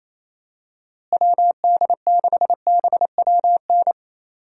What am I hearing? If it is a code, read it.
Morse code, heard as a keyed tone, about 28 wpm, 700 Hz. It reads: WD6BWN
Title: USA Call Signs - 6 Characters - Repeated 2x - 28wpm